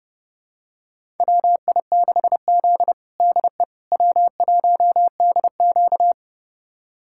Morse code, heard as a keyed tone, about 30 wpm, 700 Hz. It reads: WI6Z DE W1DQ
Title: USA Callsign DE Callsign - 4 Characters - 30wpm